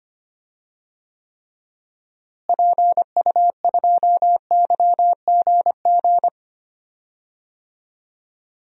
Morse code, heard as a keyed tone, about 25 wpm, 700 Hz. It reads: PU2YGG
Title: DXCC-100 Call Signs - 6 Characters - Repeated 3x - 25wpm